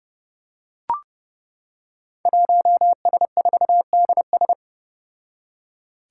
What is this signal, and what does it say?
Morse code, heard as a keyed tone, about 30 wpm, 700 Hz. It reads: 1S4DS
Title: Five Challenging Letter-Number Combinations 30wpm